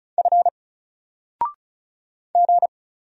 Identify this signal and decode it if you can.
Morse code, heard as a keyed tone, about 35 wpm, 700 Hz. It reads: F G
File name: Single-Letter-Number-35wpm